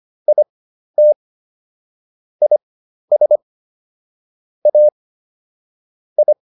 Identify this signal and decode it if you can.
Morse code, heard as a keyed tone, about 25 wpm, spaced as for 12 wpm, 600 Hz. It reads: IT IS A I